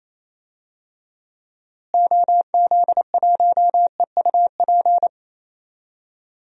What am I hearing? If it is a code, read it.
Morse code, heard as a keyed tone, about 27 wpm, 700 Hz. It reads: OZ1EUP
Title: DXCC-100 Call Signs - 6 Characters - Repeated 2x - 28wpm